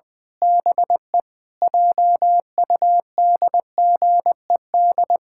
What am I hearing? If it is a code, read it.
Morse code, heard as a keyed tone, about 20 wpm, 700 Hz. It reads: BE JUDGED
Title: Pangram Challenge v2 - 20wpm